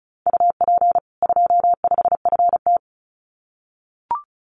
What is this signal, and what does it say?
Morse code, heard as a keyed tone, about 34 wpm, 700 Hz. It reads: UP 25FT
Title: Common QSO Phrases 35wpm